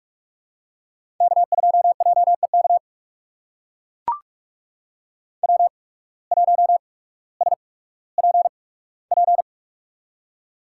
Morse code, heard as a keyed tone, about 43 wpm, spaced as for 19 wpm, 700 Hz. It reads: K2JEK W1SPP
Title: USA Call Signs - 5 Characters - 8x Character Spacing - 45wpm